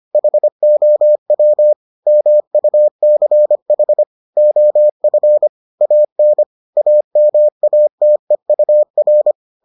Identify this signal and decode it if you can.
Morse code, heard as a keyed tone, about 25 wpm, 600 Hz. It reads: HOW MUCH OF AN AMATEUR